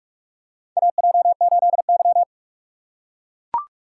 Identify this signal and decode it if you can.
Morse code, heard as a keyed tone, about 44 wpm, 690 Hz. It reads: AJ8Y
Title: USA Call Signs - 4 Characters - 2x Character Spacing - 45wpm